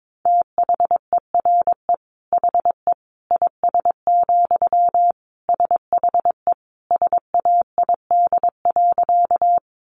Morse code, heard as a keyed tone, about 22 wpm, 700 Hz. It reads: THERE HE IS, SHE SAID.
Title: Sentences from Top 100 Words 22wpm